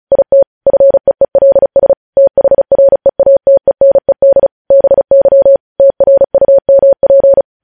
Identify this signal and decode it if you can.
Morse code, heard as a keyed tone, about 36 wpm, 550 Hz. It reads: IT FEELS THREATENED BY TRUMP
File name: podcast / News headlines-1x-35x35